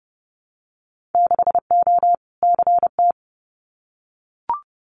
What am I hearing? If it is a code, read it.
Morse code, heard as a keyed tone, about 30 wpm, 700 Hz. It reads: 6O CT
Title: Winter Field Day Exchanges Encore 30wpm